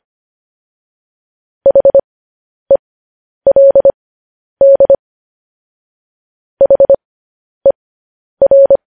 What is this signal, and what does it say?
Morse code, heard as a keyed tone, about 25 wpm, spaced as for 10 wpm, 550 Hz. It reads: HELD HER
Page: news-headlines-in-morse-code-at-10-wpm